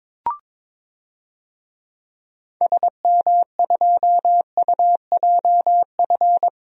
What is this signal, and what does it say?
Morse code, heard as a keyed tone, about 22 wpm, 700 Hz. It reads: SM2UJF
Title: DXCC-100 Call Signs - 6 Characters - Repeated 2x - 22wpm